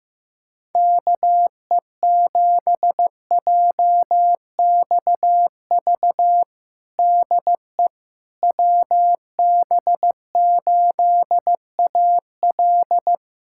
Morse code, heard as a keyed tone, about 15 wpm, 700 Hz. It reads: KE7JXV DE WB8AL
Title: USA Callsign DE Callsign 15wpm